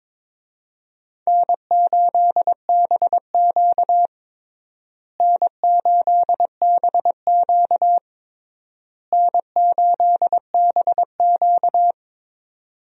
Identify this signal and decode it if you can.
Morse code, heard as a keyed tone, about 22 wpm, 700 Hz. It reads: N8BQ N8BQ N8BQ
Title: USA Call Signs - 4 Characters - Repeated 3x - 22wpm